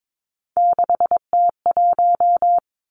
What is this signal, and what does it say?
Morse code, heard as a keyed tone, about 22 wpm, 700 Hz. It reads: 6T1
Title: Three Challenging Letter-Number Combinations 22wpm